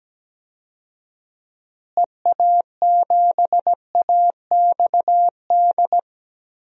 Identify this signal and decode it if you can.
Morse code, heard as a keyed tone, about 17 wpm, 700 Hz. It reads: EA7AXD
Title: DXCC-100 Call Signs - 6 Characters - Repeated 3x - 17wpm